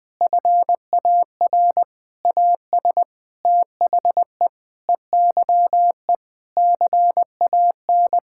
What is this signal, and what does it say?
Morse code, heard as a keyed tone, about 21 wpm, 700 Hz. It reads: FAR AS THE EYE CAN
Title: Sentences with 7-Word Phrases - 20wpm